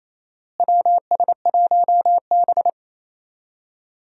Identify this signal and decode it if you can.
Morse code, heard as a keyed tone, about 28 wpm, 700 Hz. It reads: WS1B
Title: USA Call Signs - 4 Characters - Repeated 2x - 28wpm